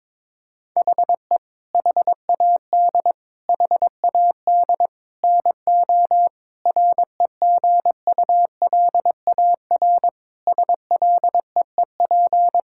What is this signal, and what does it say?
Morse code, heard as a keyed tone, about 21 wpm, 700 Hz. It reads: HE HAD HAD NO REGULAR SLEEP
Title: Sentences from Top 1000 Words - Review Part 6 - 22wpm